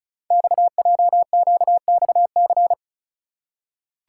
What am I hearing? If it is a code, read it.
Morse code, heard as a keyed tone, about 36 wpm, 690 Hz. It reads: XJQXC
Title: Code Groups - 5 Letters - 35wpm